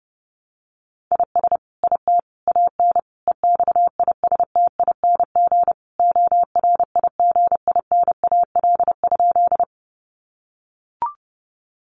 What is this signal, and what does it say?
Morse code, heard as a keyed tone, about 30 wpm, 700 Hz. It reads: IS IT AN EXISTING ORIGINAL?
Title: Sentences from Top 1500 Words 30wpm